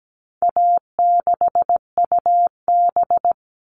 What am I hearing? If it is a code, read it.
Morse code, heard as a keyed tone, about 17 wpm, 700 Hz. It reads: A6UB
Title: All Cut Numbers - 4-Digit 17wpm